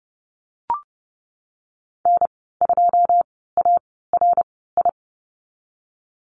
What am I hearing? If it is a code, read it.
Morse code, heard as a keyed tone, about 30 wpm, spaced as for 17 wpm, 700 Hz. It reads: N2ARI
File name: Lesson 14g - Number 2 - Call Signs 30wpm